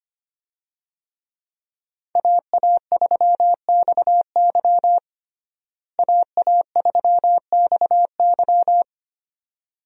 Morse code, heard as a keyed tone, about 25 wpm, 700 Hz. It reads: AA3XY AA3XY
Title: USA Call Signs - 5 Characters - Repeated 2x - 25wpm